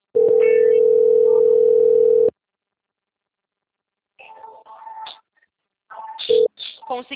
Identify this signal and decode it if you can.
Ringback tone (North America), 440 + 480 Hz, with speech.
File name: out- - - - - .43076